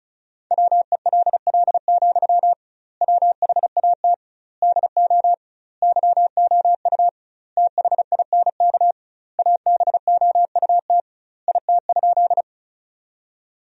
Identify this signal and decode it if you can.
Morse code, heard as a keyed tone, about 34 wpm, 700 Hz. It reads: WELL, WHAT DO YOU THINK ABOUT IT?
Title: 2x Word-Spacing - Sentences from Top 200 Words - 35wpm